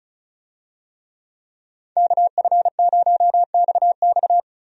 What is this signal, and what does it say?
Morse code, heard as a keyed tone, about 34 wpm, 700 Hz. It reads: KF0XX
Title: USA Call Signs - 5 Characters - Repeated 2x - 35wpm